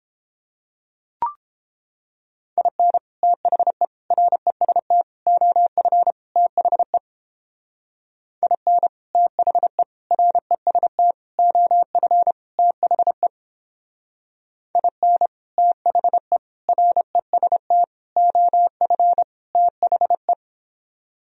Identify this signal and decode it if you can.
Morse code, heard as a keyed tone, about 29 wpm, 700 Hz. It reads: IN THE REST OF THE IN THE REST OF THE IN THE REST OF THE